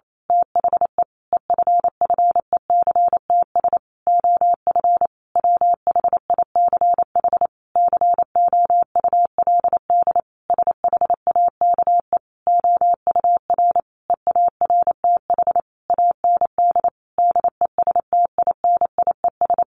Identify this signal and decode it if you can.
Morse code, heard as a keyed tone, about 28 wpm, 700 Hz. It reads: THE EFFECTS OF WHICH COULD SHAKE OUR EARTH AND DESTINIES
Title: A Sound Of Thunder 28wpm